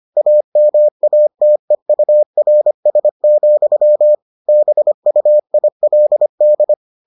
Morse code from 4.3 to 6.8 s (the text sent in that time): BUILD